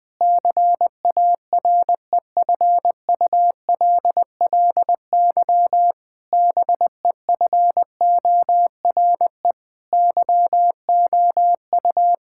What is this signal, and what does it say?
Morse code, heard as a keyed tone, about 21 wpm, 700 Hz. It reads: CAREFULLY BEFORE YOU
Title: Sentences from Top 1600 Words 20wpm